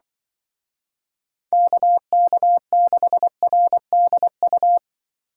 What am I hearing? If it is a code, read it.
Morse code, heard as a keyed tone, about 24 wpm, 700 Hz. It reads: KK6RDU
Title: Speed Racing - USA 2x3 Call Signs 20wpm